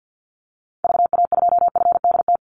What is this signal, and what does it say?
Morse code, heard as a keyed tone, about 51 wpm, 700 Hz. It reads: VA2FDT
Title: Call Signs - 50wpm